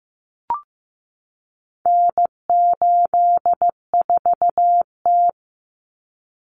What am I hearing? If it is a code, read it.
Morse code, heard as a keyed tone, about 15 wpm, 700 Hz. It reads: N84T
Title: Common Cut Numbers - 4-Digit 15wpm